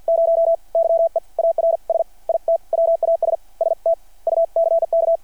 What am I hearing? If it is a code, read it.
Morse code, heard as a keyed tone, about 52 wpm, spaced as for 39 wpm, 660 Hz. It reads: 0 YEARS IT WAS STUCK
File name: archived / CRSnet